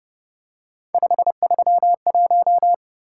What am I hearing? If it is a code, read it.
Morse code, heard as a keyed tone, about 29 wpm, 700 Hz. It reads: 531
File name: Speed-Racing - Three Letter-Number Combinations - 25wpm